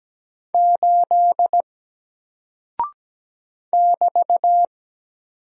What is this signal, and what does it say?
Morse code, heard as a keyed tone, about 17 wpm, 700 Hz. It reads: 8 <BT>
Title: Single Character (Letter:Number:Punctuation:Prosign) Alt 17wpm